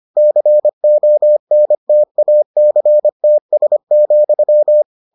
Morse code, heard as a keyed tone, about 25 wpm, 600 Hz. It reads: CONTACTS,